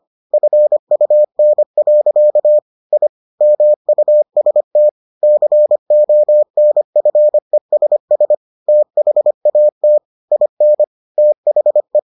Morse code, heard as a keyed tone, about 25 wpm, 600 Hz. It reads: FUN. I MUST CONFESS THAT IN THE